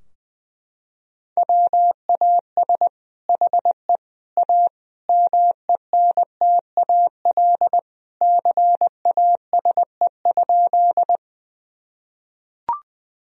Morse code, heard as a keyed tone, about 21 wpm, 700 Hz. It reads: WAS HE A MENTAL CASE?
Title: Sentences from Top 1000 Words - Review Part 4 - 20wpm